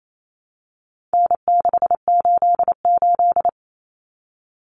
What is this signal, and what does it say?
Morse code, heard as a keyed tone, about 28 wpm, 700 Hz. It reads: N688
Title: Common Cut Numbers - 4-Digit 28wpm